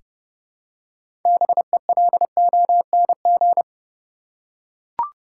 Morse code, heard as a keyed tone, about 32 wpm, 700 Hz. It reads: BELONG
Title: Six Letter Words - 30wpm